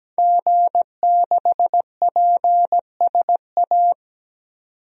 Morse code, heard as a keyed tone, about 17 wpm, 700 Hz. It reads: G6PSA
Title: DXCC-100 Call Signs - 5 Characters - 17wpm